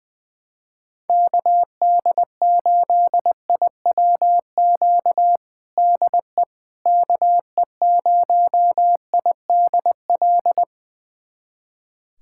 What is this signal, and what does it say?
Morse code, heard as a keyed tone, about 21 wpm, 700 Hz. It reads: KD8IWQ DE KE0IDL
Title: USA Callsign DE Callsign - 6 Characters - 20wpm